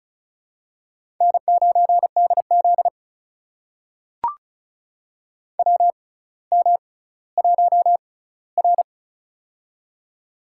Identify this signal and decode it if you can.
Morse code, heard as a keyed tone, about 36 wpm, spaced as for 18 wpm, 700 Hz. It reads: N9DZ WM1R
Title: USA Call Signs - 4 Characters - 6x Character Spacing - 35wpm